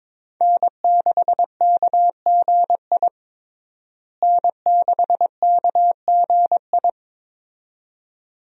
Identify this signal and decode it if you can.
Morse code, heard as a keyed tone, about 22 wpm, 700 Hz. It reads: N6KGI N6KGI
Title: USA Call Signs - 5 Characters - Repeated 2x - 22wpm